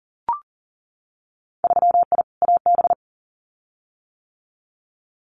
Morse code, heard as a keyed tone, about 39 wpm, 700 Hz. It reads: 3I AB
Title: Winter Field Day Exchanges Encore 40wpm